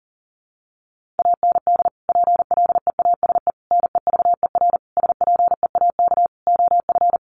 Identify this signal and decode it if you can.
Morse code, heard as a keyed tone, about 40 wpm, 700 Hz. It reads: AND PLEASE NEVER SPEAK OF
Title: Sentences from Top 1500 Words 40wpm